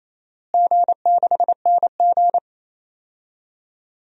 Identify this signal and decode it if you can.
Morse code, heard as a keyed tone, about 28 wpm, 700 Hz. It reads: G6NG
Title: DXCC-100 Call Signs - 4 Characters - Repeated 3x - 28wpm